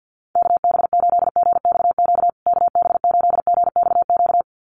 Morse code, heard as a keyed tone, about 50 wpm, 700 Hz. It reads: KB8GXQ KB8GXQ
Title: USA Call Signs Repeated 2x 50wpm